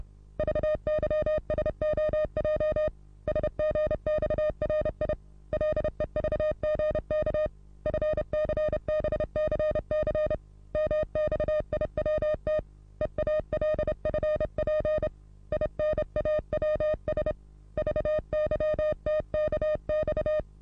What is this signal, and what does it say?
Morse code, heard as a keyed tone, about 29 wpm, 610 Hz. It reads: VYSOJ SGXRI LEVGK FCBCC MXIWT EALFP INAWS VYTKX
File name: repos / Morsekurs / DJ4UF Morsen27